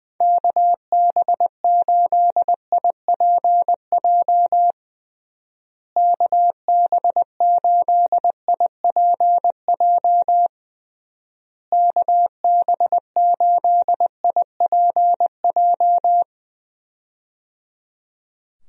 Morse code, heard as a keyed tone, about 21 wpm, 700 Hz. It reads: KB8IPJ KB8IPJ KB8IPJ
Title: USA Call Signs - 6 Characters - Repeated 3x - 20wpm